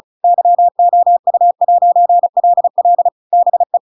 Morse code, heard as a keyed tone, about 34 wpm, 700 Hz. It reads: YOU'LL BE